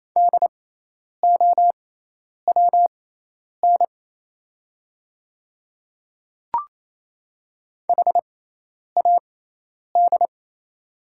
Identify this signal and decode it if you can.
Morse code, heard as a keyed tone, about 28 wpm, spaced as for 10 wpm, 690 Hz. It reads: DOWN HAD
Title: 6x Character Spacing - Top 100 Words - 28wpm